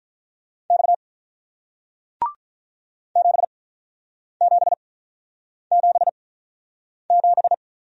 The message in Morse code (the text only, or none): X 7777